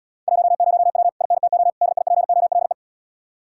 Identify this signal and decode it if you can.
Morse code, heard as a keyed tone, about 79 wpm, 700 Hz. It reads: 10W INTO DIPOLE